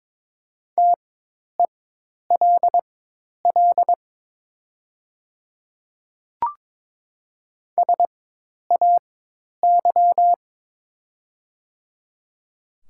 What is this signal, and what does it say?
Morse code, heard as a keyed tone, about 22 wpm, spaced as for 10 wpm, 700 Hz. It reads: TELL SAY
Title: 4x Character Spacing - Top 200 Words - 22wpm